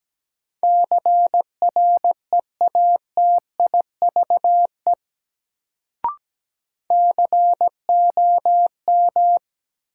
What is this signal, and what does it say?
Morse code, heard as a keyed tone, about 17 wpm, 700 Hz. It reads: CREATIVE COM
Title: Sentences from Top 1300 Words - New Word Drill Alt - 17wpm